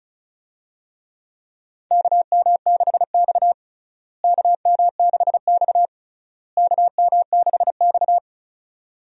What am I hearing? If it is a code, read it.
Morse code, heard as a keyed tone, about 36 wpm, 700 Hz. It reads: KM6X KM6X KM6X
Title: USA Call Signs - 4 Characters - Repeated 3x - 35wpm